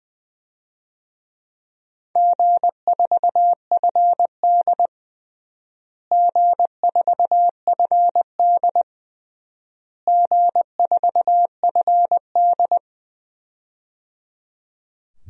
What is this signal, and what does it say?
Morse code, heard as a keyed tone, about 20 wpm, 700 Hz. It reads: G4FD G4FD G4FD
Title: DXCC-100 Call Signs - 4 Characters - Repeated 3x - 20wpm